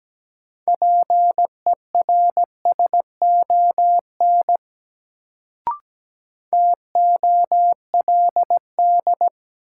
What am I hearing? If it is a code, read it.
Morse code, heard as a keyed tone, about 17 wpm, 700 Hz. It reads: PERSON TOLD
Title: Sentences from Top 400 Words - New Word Drill - 17wpm